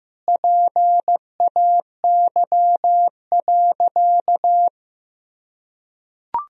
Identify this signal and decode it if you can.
Morse code, heard as a keyed tone, about 15 wpm, 700 Hz. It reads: PAY.
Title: Sentences from Top 1000 Words - Review Part 4 - 15wpm